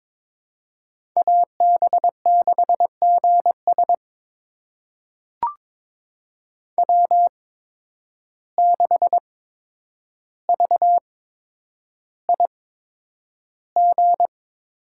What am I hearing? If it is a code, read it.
Morse code, heard as a keyed tone, about 22 wpm, spaced as for 9 wpm, 700 Hz. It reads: AB6GS W6VIG
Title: USA Call Signs - 5 Characters - 8x Character Spacing - 22wpm